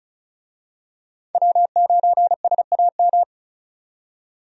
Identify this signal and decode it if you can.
Morse code, heard as a keyed tone, about 36 wpm, 700 Hz. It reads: W9SAM
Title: USA Call Signs - 5 Characters - 35wpm